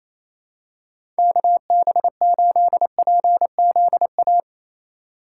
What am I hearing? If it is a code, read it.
Morse code, heard as a keyed tone, about 28 wpm, 700 Hz. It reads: KB8PZA
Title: USA Call Signs - 6 Characters - 3x Character Spacing - 28wpm